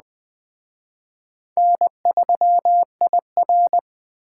Morse code, heard as a keyed tone, about 20 wpm, 700 Hz. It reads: N3IR